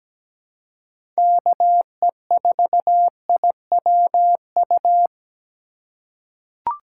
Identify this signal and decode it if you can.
Morse code, heard as a keyed tone, about 17 wpm, 700 Hz. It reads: KE4IWU
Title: USA Call Signs - 6 Characters - 2x Character Spacing - 17wpm